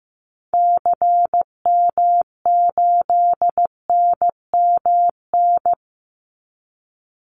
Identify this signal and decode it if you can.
Morse code, heard as a keyed tone, about 15 wpm, 700 Hz. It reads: CM8NMN
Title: Call Signs Repeated 3x - 15wpm